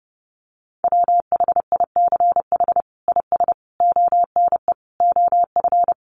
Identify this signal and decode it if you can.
Morse code, heard as a keyed tone, about 30 wpm, 700 Hz. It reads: WHICH IS ONE OF
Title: Sets of 5 Words Encore 30wpm